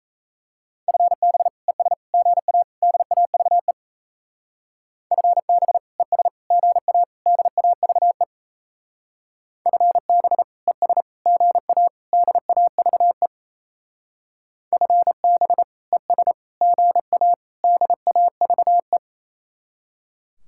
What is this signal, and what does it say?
Morse code, heard as a keyed tone, about 37 wpm, 700 Hz. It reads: FB ES GA DAVE FB ES GA DAVE FB ES GA DAVE FB ES GA DAVE